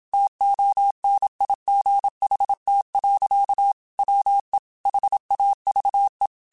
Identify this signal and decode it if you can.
Morse code, heard as a keyed tone, about 27 wpm, 800 Hz. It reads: TONIGHT. WE HAVE
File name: QSOs / QSOs-mp3s / WD0BGP 1@25